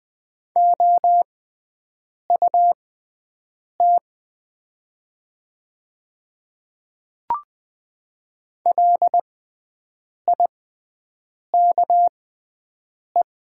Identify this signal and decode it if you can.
Morse code, heard as a keyed tone, about 20 wpm, spaced as for 7 wpm, 700 Hz. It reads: OUT LIKE
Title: 6x Character Spacing - Top 300 Words - 20wpm